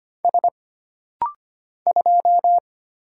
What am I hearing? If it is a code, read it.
Morse code, heard as a keyed tone, about 25 wpm, 700 Hz. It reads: S 2
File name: Single Character (Letter:Number:Punctuation:Prosign) - Rapid-Fire - 25wpm